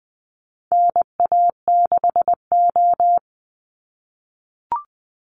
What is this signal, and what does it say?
Morse code, heard as a keyed tone, about 19 wpm, 700 Hz. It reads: NA6O
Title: NA SOTA Chaser Callsigns 20wpm